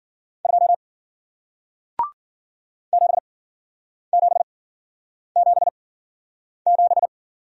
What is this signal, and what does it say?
Morse code, heard as a keyed tone, about 53 wpm, spaced as for 9 wpm, 700 Hz. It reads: 2 7777